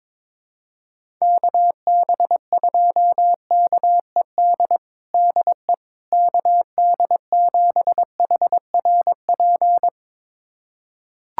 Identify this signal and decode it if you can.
Morse code, heard as a keyed tone, about 22 wpm, 700 Hz. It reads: KB2KED DE KD7HRP
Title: USA Callsign DE Callsign 22wpm